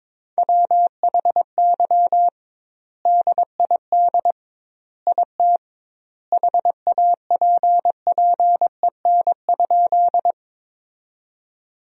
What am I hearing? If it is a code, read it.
Morse code, heard as a keyed tone, about 22 wpm, 700 Hz. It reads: WHY DID IT HAPPEN?
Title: 2x Word-Spacing - Sentences from Top 400 Words - 22wpm